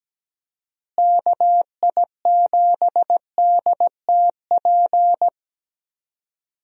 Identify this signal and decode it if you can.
Morse code, heard as a keyed tone, about 17 wpm, 700 Hz. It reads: KI7DTP
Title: USA Call Signs - 6 Characters - Repeated 3x - 17wpm